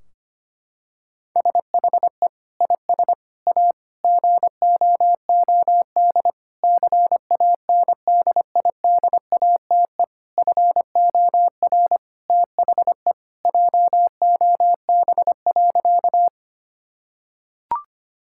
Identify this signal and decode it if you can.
Morse code, heard as a keyed tone, about 25 wpm, 700 Hz. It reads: SHE IS A GOOD CANDIDATE FOR THE JOB.
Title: Sentences from Top 1600 Words 25wpm